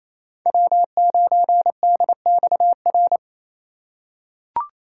Morse code, heard as a keyed tone, about 28 wpm, 700 Hz. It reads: W9DXR
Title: K1USN SST QSO Practice - 28wpm